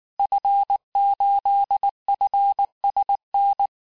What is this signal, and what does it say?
Morse code, heard as a keyed tone, about 19 wpm, 800 Hz. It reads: F8FSN
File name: cours-du-lundi files / cours-1583-fin-QSO-18-mpm